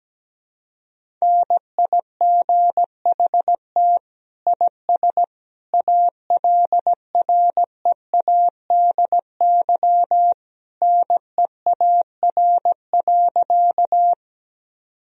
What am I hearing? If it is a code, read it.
Morse code, heard as a keyed tone, about 17 wpm, 700 Hz. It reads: NIGHT IS ALREADY NEAR.
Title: Sentences from Top 1000 Words - Review Part 1 - 17wpm